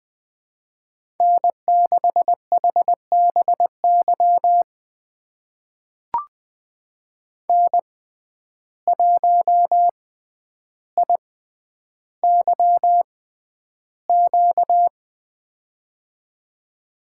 Morse code, heard as a keyed tone, about 21 wpm, spaced as for 10 wpm, 700 Hz. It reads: N6HBY N1IYQ
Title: USA Call Signs - 6x Character Spacing - 20wpm